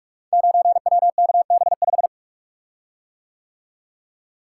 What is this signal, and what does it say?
Morse code, heard as a keyed tone, about 44 wpm, 700 Hz. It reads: 9WKB5
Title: Code Groups - 5 Letter:Numbers - 45wpm